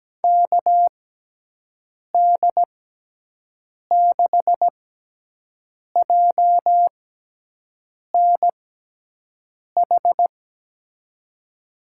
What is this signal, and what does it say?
Morse code, heard as a keyed tone, about 17 wpm, spaced as for 6 wpm, 700 Hz. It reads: KD6JNH
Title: USA Call Signs - 6 Characters - 6x Character Spacing - 17wpm